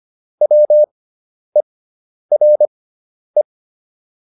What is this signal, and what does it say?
Morse code, heard as a keyed tone, about 25 wpm, spaced as for 10 wpm, 600 Hz. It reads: WERE
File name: SS213 10